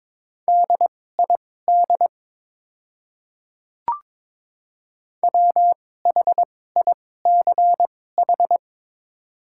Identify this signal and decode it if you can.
Morse code, heard as a keyed tone, about 22 wpm, spaced as for 16 wpm, 700 Hz. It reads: DID WHICH
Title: 2x Character Spacing - Top 200 Words - 22wpm